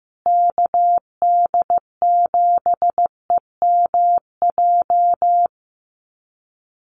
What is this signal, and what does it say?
Morse code, heard as a keyed tone, about 15 wpm, 700 Hz. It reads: KD7EMJ
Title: USA Call Signs Repeated 3x 15wpm